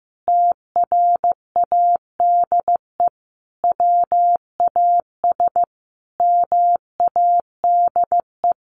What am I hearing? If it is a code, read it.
Morse code, heard as a keyed tone, about 14 wpm, 700 Hz. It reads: TRADE WAS MADE
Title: Sentences from Top 1400 Words 15wpm